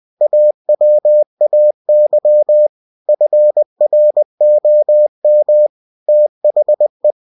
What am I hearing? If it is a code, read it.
Morse code, heard as a keyed tone, about 20 wpm, 600 Hz. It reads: AWAY FROM THE